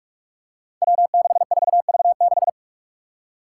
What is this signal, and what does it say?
Morse code, heard as a keyed tone, about 44 wpm, 700 Hz. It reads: W64V6
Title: Five Challenging Letter-Number Combinations 45wpm